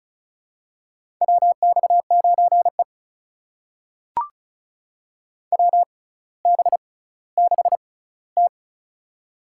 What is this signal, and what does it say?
Morse code, heard as a keyed tone, about 36 wpm, spaced as for 18 wpm, 700 Hz. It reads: WX9E WB6T
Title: USA Call Signs - 4 Characters - 6x Character Spacing - 35wpm